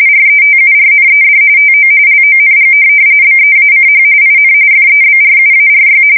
RTTY, about 45.45 baud. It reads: RETURNED TO LONG BEACH IN FEBRUARY
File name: fsk170 / USS-New-Jersey